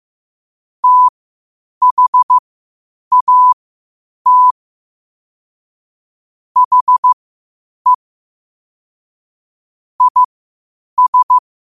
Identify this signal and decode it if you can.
Morse code, heard as a keyed tone, about 15 wpm, spaced as for 8 wpm, 1000 Hz. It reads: THAT HE IS
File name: qotdfiles / 8wpm